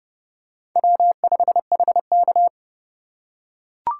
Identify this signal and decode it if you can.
Morse code, heard as a keyed tone, about 29 wpm, 700 Hz. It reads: W5HK